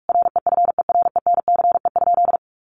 Morse code, heard as a keyed tone, about 44 wpm, 700 Hz. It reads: REFERENCE?
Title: Sentences from Top 1200 Words 45wpm